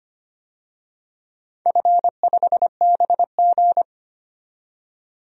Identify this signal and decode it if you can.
Morse code, heard as a keyed tone, about 25 wpm, 700 Hz. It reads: F5BG
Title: DXCC-100 Call Signs - 4 Characters - 25wpm